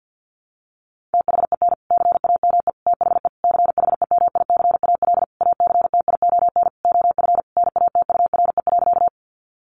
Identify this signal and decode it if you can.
Morse code, heard as a keyed tone, about 50 wpm, 700 Hz. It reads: THEN CAME THE CHEMICAL ACTION OF NATURE.
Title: Sentences from Top 1200 Words 50wpm